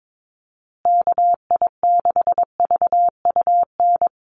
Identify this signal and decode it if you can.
Morse code, heard as a keyed tone, about 22 wpm, 700 Hz. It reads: KI6VUN
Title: USA Call Signs Repeated 2x 22wpm